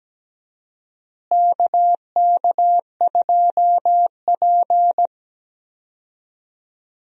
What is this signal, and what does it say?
Morse code, heard as a keyed tone, about 17 wpm, 700 Hz. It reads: KK2P
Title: USA Call Signs - 4 Characters - Repeated 3x - 17wpm